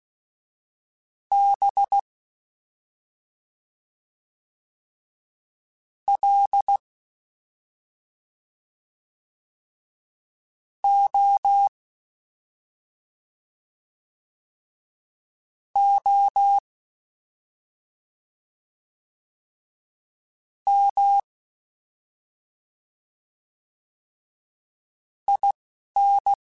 Morse code, heard as a keyed tone, about 16 wpm, 800 Hz. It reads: B L O O M I N
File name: Special INGs-mp3s / Special INGs-mp3s / Slow ing @10